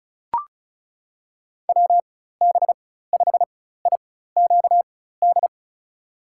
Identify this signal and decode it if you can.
Morse code, heard as a keyed tone, about 36 wpm, spaced as for 16 wpm, 700 Hz. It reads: WB5IQD
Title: USA Call Signs - 6 Characters - 4x Character Spacing - 35wpm